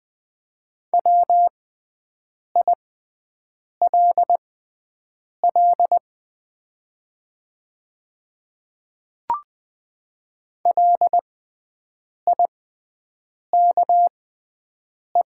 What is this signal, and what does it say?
Morse code, heard as a keyed tone, about 20 wpm, spaced as for 7 wpm, 700 Hz. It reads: WILL LIKE
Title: 6x Character Spacing - Top 100 Words - 20wpm